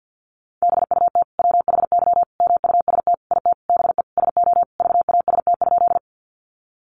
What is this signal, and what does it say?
Morse code, heard as a keyed tone, about 50 wpm, 700 Hz. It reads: BUT WHY MUST IT BE SO VAST?
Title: Sentences from Top 1300 Words 50wpm